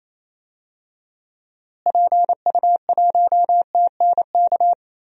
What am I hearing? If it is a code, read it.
Morse code, heard as a keyed tone, about 28 wpm, 700 Hz. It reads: PU1TNK
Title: DXCC-100 Call Signs - 6 Characters - Repeated 3x - 28wpm